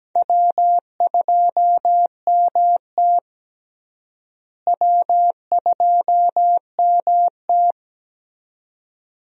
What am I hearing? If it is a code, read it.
Morse code, heard as a keyed tone, about 17 wpm, 700 Hz. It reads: W2MT W2MT
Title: USA Call Signs - 4 Characters - Repeated 2x - 17wpm